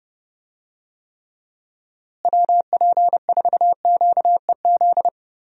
Morse code, heard as a keyed tone, about 30 wpm, 700 Hz. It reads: WP4QEZ
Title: USA Call Signs - 6 Characters - Repeated 2x - 30wpm